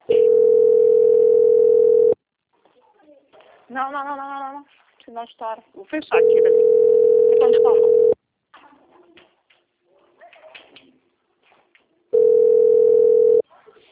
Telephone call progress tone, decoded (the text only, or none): ringback tone